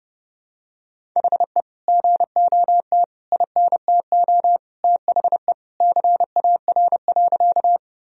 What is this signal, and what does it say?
Morse code, heard as a keyed tone, about 30 wpm, 700 Hz. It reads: HE GOT INTO THE CAR.
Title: Sentences from Top 1000 Words - Review Part 1 - 30wpm